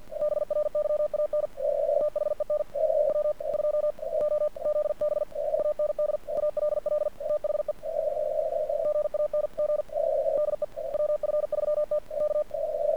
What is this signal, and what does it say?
Morse code, heard as a keyed tone, about 48 wpm, 1200 Hz. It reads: BRYAN THEN W2OZB AND ALL THE GANG DE WF3T K
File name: archived / CRSnet